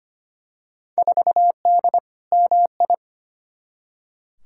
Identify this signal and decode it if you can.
Morse code, heard as a keyed tone, about 25 wpm, 700 Hz. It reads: 4D MI
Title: Field Day Exchanges Encore - 25wpm